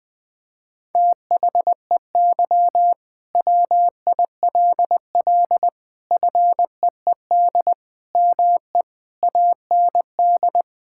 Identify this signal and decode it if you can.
Morse code, heard as a keyed tone, about 21 wpm, 700 Hz. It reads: THEY WILL FEED ME AND
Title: Sentences from Top 1600 Words 20wpm